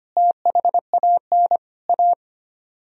Morse code, heard as a keyed tone, about 32 wpm, 700 Hz. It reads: THAN A